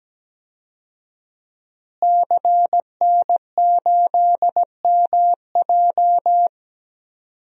Morse code, heard as a keyed tone, about 17 wpm, 700 Hz. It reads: CN8MJ
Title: DXCC-100 Call Signs - 5 Characters - Repeated 2x - 17wpm